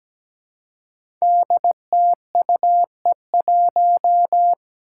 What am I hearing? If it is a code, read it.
Morse code, heard as a keyed tone, about 17 wpm, 700 Hz. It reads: DTUE1
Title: Five Challenging Letter-Number Combinations 17wpm